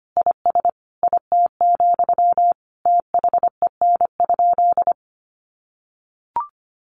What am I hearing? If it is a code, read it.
Morse code, heard as a keyed tone, about 25 wpm, 700 Hz. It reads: IS IT, THEN?
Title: Sentences from Top 200 Words 25wpm